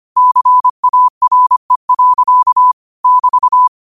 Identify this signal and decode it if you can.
Morse code, heard as a keyed tone, about 25 wpm, 1000 Hz. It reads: CARE. <BT>
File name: 25wpm